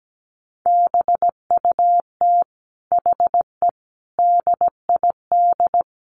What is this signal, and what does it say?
Morse code, heard as a keyed tone, about 17 wpm, 700 Hz. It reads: BUT HE DID
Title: Sentences from Top 600 Words 17wpm